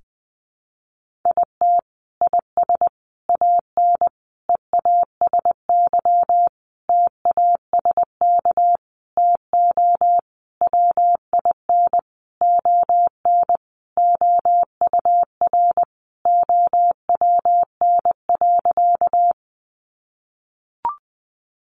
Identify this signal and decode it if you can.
Morse code, heard as a keyed tone, about 20 wpm, 700 Hz. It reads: IT IS AN EASY TASK TO WIN ON OUR OWN.
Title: Sentences from Top 1500 Words 20wpm